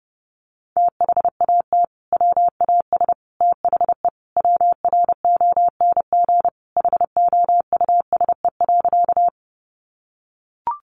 Morse code, heard as a keyed tone, about 30 wpm, 700 Hz. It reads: THAT WAS THE WRONG HOUSE.
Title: Sentences from Top 900 Words 30wpm